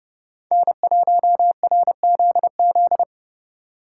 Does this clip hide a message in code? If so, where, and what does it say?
Yes: Morse code from 0.5 to 3.0 s, heard as a keyed tone, about 28 wpm, 700 Hz. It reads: N1RZZ